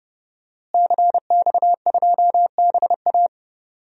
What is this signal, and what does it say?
Morse code, heard as a keyed tone, about 30 wpm, 700 Hz. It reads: CX2BA
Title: DXCC-100 Call Signs - 5 Characters - Repeated 2x - 30wpm